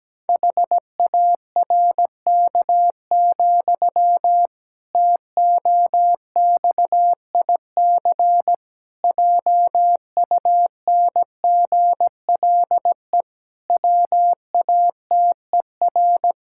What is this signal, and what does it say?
Morse code, heard as a keyed tone, about 17 wpm, 700 Hz. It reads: HARK, TOXIC JUNGLE WATER
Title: Pangram Challenge v2 - 17wpm